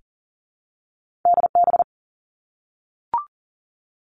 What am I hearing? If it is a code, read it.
Morse code, heard as a keyed tone, about 42 wpm, 700 Hz. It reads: DB
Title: Two Challenging Letter-Number Combinations 40wpm